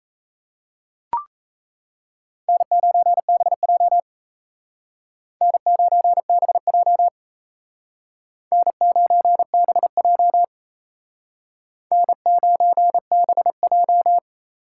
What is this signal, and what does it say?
Morse code, heard as a keyed tone, about 35 wpm, 700 Hz. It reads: N9BJ N9BJ N9BJ N9BJ